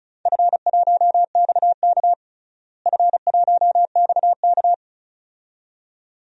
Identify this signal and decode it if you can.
Morse code, heard as a keyed tone, about 36 wpm, 700 Hz. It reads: F1XK F1XK
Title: DXCC-100 Call Signs - 4 Characters - Repeated 2x - 35wpm